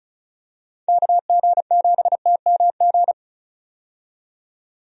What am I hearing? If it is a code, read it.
Morse code, heard as a keyed tone, about 36 wpm, 700 Hz. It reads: KG7TMG
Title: USA Call Signs - 6 Characters - 35wpm